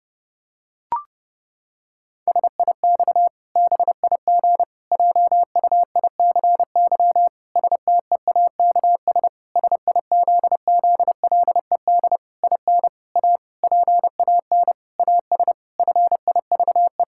Morse code, heard as a keyed tone, about 30 wpm, 700 Hz. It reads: SIX BIG JUICY STEAKS SIZZLED IN A PAN AS FIVE
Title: Pangram Challenge v2 - 30wpm